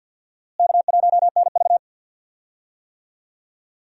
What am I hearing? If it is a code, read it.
Morse code, heard as a keyed tone, about 52 wpm, 700 Hz. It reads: K1NV
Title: USA Call Signs - 4 Characters - Repeated 3x - 50wpm